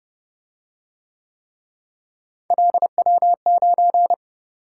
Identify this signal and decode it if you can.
Morse code, heard as a keyed tone, about 29 wpm, 700 Hz. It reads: LW9